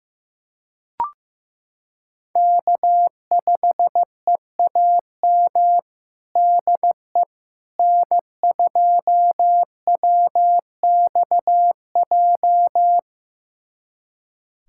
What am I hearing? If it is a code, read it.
Morse code, heard as a keyed tone, about 15 wpm, 700 Hz. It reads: K5EAM DE N2WXJ
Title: USA Callsign DE Callsign - 5 Characters - 15wpm